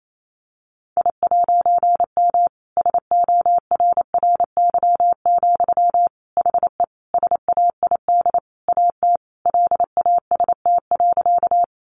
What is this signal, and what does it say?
Morse code, heard as a keyed tone, about 28 wpm, 700 Hz. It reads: I'M SORRY, HE SAID AT LAST.
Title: A Sound Of Thunder 28wpm